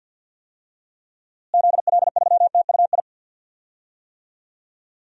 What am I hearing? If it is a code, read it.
Morse code, heard as a keyed tone, about 52 wpm, 700 Hz. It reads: ZL3TUI
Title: DXCC-100 Call Signs - 6 Characters - Repeated 2x - 50wpm